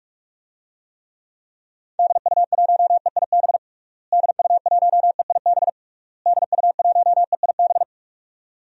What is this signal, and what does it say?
Morse code, heard as a keyed tone, about 46 wpm, 700 Hz. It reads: DU1EIB DU1EIB DU1EIB
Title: DXCC-100 Call Signs - 6 Characters - Repeated 3x - 45wpm